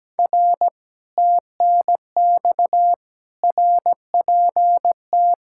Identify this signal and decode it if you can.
Morse code, heard as a keyed tone, about 17 wpm, 700 Hz. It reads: R TNX RPT
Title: Speed-Racing Common QSO Phrases Encore - 17wpm